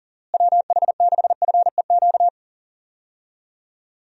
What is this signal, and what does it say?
Morse code, heard as a keyed tone, about 40 wpm, 700 Hz. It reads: WH6FEQ
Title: USA Call Signs - 6 Characters - Repeated 2x - 40wpm